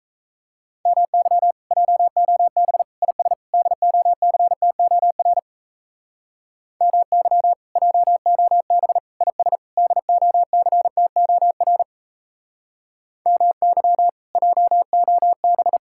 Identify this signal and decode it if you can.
Morse code, heard as a keyed tone, about 35 wpm, 700 Hz. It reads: MY JOB IS DOCTOR MY JOB IS DOCTOR MY JOB
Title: Speed-Racing Common QSO Phrases - 28wpm